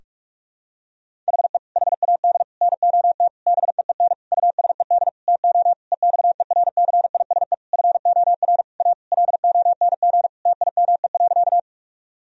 Morse code, heard as a keyed tone, about 41 wpm, 700 Hz. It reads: HE HAD NOT BEEN USED TO EXERCISE FOR A LONG TIME.
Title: Sentences from Top 1000 Words - Review Part 4 - 45wpm